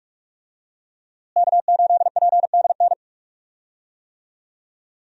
Morse code, heard as a keyed tone, about 46 wpm, 700 Hz. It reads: K8PDN
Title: USA Call Signs - 5 Characters - Repeated 2x - 45wpm